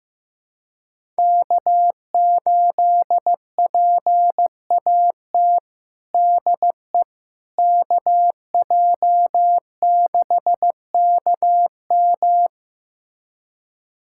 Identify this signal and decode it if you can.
Morse code, heard as a keyed tone, about 15 wpm, 700 Hz. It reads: K8PAT DE KJ6KM
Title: USA Callsign DE Callsign - 5 Characters - 15wpm